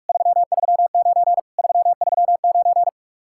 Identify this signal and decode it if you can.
Morse code, heard as a keyed tone, about 54 wpm, 700 Hz. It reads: 339 339
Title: Speed-Racing Common QSO Phrases - 45wpm